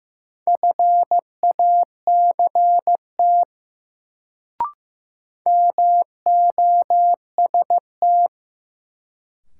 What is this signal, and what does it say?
Morse code, heard as a keyed tone, about 15 wpm, 700 Hz. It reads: FACT MOST
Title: Sets of 2 Words - New Word Drill - 15wpm